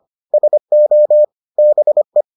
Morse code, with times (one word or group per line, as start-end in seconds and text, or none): 0.3-1.2 SO
1.6-2.2 BE